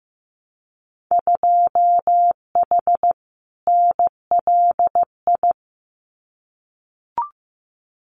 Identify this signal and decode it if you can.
Morse code, heard as a keyed tone, about 15 wpm, 700 Hz. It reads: 2H NLI
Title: Winter Field Day Exchanges Encore 15wpm